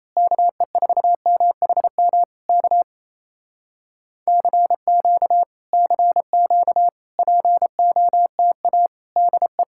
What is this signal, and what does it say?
Morse code, heard as a keyed tone, about 28 wpm, 700 Hz. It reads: KE4MHM K CQ CQ POTA DE
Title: Speed-Racing Common QSO Phrases Encore - 28wpm